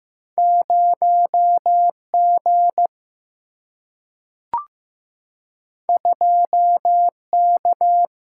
Morse code, heard as a keyed tone, about 15 wpm, 700 Hz. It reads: 0 G 2 K
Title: Single-Letter Single-Number 15wpm